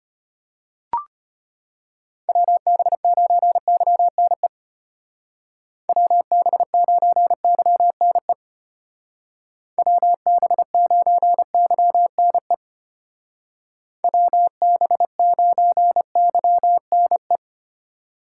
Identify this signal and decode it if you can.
Morse code, heard as a keyed tone, about 32 wpm, 700 Hz. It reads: WB9YNE WB9YNE WB9YNE WB9YNE